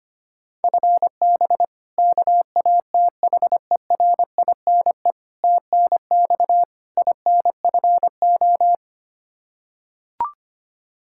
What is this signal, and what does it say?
Morse code, heard as a keyed tone, about 25 wpm, 700 Hz. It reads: FB KATHERINE TNX INFO
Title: Practice QSO - Exchange 3 - More Info - 25wpm